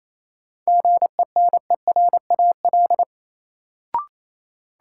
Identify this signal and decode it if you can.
Morse code, heard as a keyed tone, about 28 wpm, 700 Hz. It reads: GENERAL
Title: Sentences from Top 600 Words - New Word Drill - 28wpm